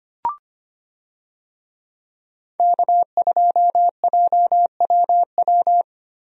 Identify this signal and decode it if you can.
Morse code, heard as a keyed tone, about 25 wpm, 700 Hz. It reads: K2JWW
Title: USA Call Signs - 5 Characters - 25wpm